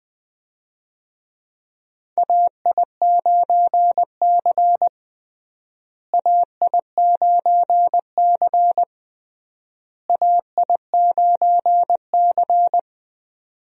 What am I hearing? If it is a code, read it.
Morse code, heard as a keyed tone, about 21 wpm, 700 Hz. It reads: AI9C AI9C AI9C
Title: USA Call Signs - 4 Characters - Repeated 3x - 20wpm